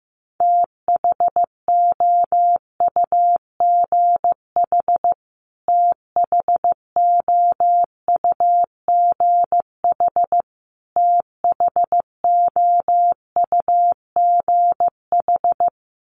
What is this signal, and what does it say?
Morse code, heard as a keyed tone, about 15 wpm, 700 Hz. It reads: THOUGH THOUGH THOUGH
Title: Top 500 Words Repeated 15wpm